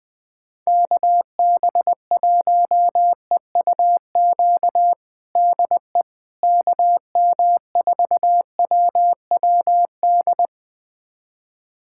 Morse code, heard as a keyed tone, about 20 wpm, 700 Hz. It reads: KB1EUQ DE KM4WWD
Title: USA Callsign DE Callsign - 6 Characters - 20wpm